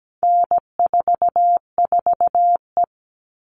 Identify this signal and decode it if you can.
Morse code, heard as a keyed tone, about 17 wpm, 700 Hz. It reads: N44E
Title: All Cut Numbers - 4-Digit 17wpm